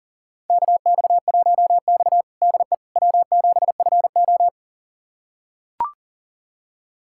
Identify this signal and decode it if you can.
Morse code, heard as a keyed tone, about 40 wpm, 700 Hz. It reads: KX1X DE W7FO
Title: USA Callsign DE Callsign - 4 Characters - 40wpm